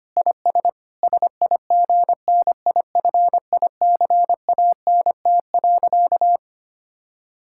Morse code, heard as a keyed tone, about 25 wpm, 700 Hz. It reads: IS SIGNIFICANT.
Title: Sentences from Top 1000 Words - Review Part 6 - 25wpm